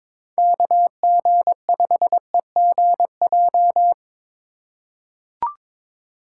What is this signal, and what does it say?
Morse code, heard as a keyed tone, about 21 wpm, 700 Hz. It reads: KG5EGJ
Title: USA Call Signs - 6 Characters - 2x Character Spacing - 22wpm